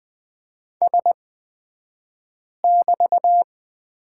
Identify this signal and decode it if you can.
Morse code, heard as a keyed tone, about 20 wpm, spaced as for 5 wpm, 700 Hz. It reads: S<BT>
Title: 223 Lesson 36d - BT - Two Characters 20wpm